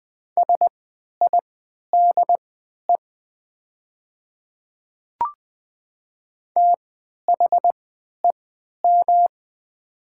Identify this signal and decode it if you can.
Morse code, heard as a keyed tone, about 20 wpm, spaced as for 11 wpm, 700 Hz. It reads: SIDE THEM
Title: 3x Character Spacing - Top 300 Words - 20wpm